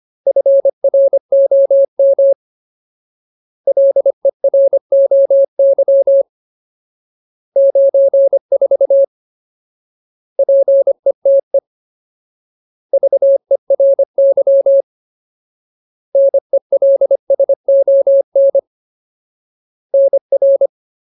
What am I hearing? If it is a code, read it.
Morse code, heard as a keyed tone, about 25 wpm, 550 Hz. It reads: FROM LEROY 94 PETE VERY NELSON NR
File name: Ultimate Word List / 25WPM Ultimate